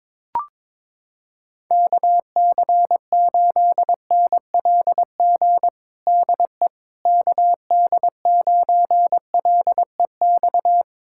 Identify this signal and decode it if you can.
Morse code, heard as a keyed tone, about 22 wpm, 700 Hz. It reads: KC8NLG DE KD9LEX
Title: USA Callsign DE Callsign - 6 Characters - 22wpm